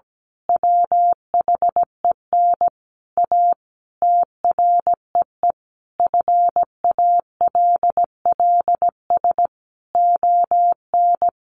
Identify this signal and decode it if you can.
Morse code, heard as a keyed tone, about 17 wpm, 700 Hz. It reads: WHEN A TREE FALLS ON
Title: A Sound Of Thunder 17wpm